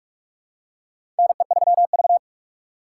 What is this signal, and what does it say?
Morse code, heard as a keyed tone, about 41 wpm, 700 Hz. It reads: NE3V
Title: USA Call Signs - 4 Characters - 3x Character Spacing - 45wpm